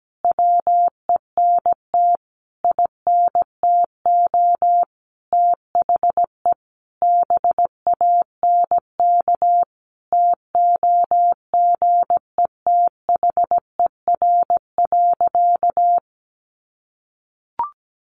Morse code, heard as a keyed tone, about 17 wpm, 700 Hz. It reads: WENT INTO THE BANK TOGETHER.
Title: Sentences from Top 900 Words 17wpm